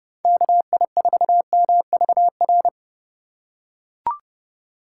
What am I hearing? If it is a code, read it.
Morse code, heard as a keyed tone, about 28 wpm, 700 Hz. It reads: KI4MVR